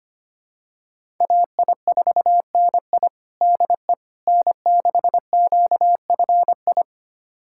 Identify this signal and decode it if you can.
Morse code, heard as a keyed tone, about 25 wpm, 700 Hz. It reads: AI4NI DE N6QFI
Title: USA Callsign DE Callsign - 5 Characters - 25wpm